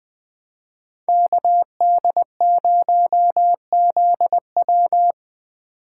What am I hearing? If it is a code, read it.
Morse code, heard as a keyed tone, about 25 wpm, 700 Hz. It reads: KD0ZW